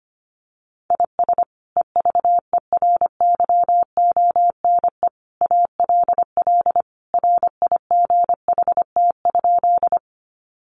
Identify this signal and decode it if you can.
Morse code, heard as a keyed tone, about 25 wpm, 700 Hz. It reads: IS EVERYONE ALL RIGHT?
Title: Sentences from Top 800 Words 25wpm